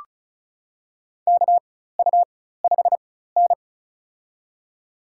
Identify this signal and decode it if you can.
Morse code, heard as a keyed tone, about 36 wpm, spaced as for 16 wpm, 700 Hz. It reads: KU5N
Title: USA Call Signs - 4 Characters - 4x Character Spacing - 35wpm